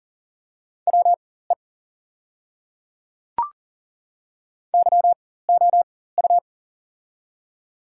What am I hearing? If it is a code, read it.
Morse code, heard as a keyed tone, about 40 wpm, spaced as for 19 wpm, 700 Hz. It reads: WE YOU
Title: 4x Character Spacing - Top 100 Words - 40wpm